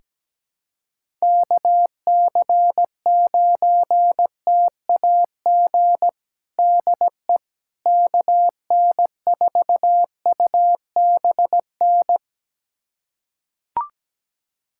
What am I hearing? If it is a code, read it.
Morse code, heard as a keyed tone, about 17 wpm, 700 Hz. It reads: KC9TAG DE KN4UBN
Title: USA Callsign DE Callsign - 6 Characters - 17wpm